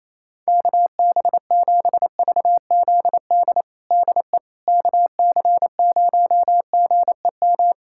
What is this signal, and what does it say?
Morse code, heard as a keyed tone, about 28 wpm, 700 Hz. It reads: KB7VZD DE KC0GEM
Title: USA Callsign DE Callsign 28wpm